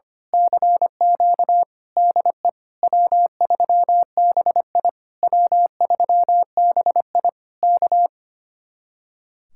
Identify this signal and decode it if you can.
Morse code, heard as a keyed tone, about 24 wpm, 700 Hz. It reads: CQ DE W3BI W3BI K
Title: Speed-Racing Common QSO Phrases - 25wpm